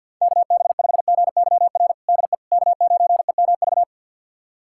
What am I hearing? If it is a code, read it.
Morse code, heard as a keyed tone, about 52 wpm, 700 Hz. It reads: KB5GYR DE K9EMV
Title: USA Callsign DE Callsign 50wpm